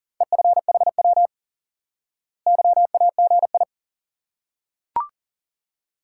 Morse code, heard as a keyed tone, about 38 wpm, 700 Hz. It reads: EFHW YAGI
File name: IQE Course 3 - Lesson 89 - YAGI - Two Elements - 40wpm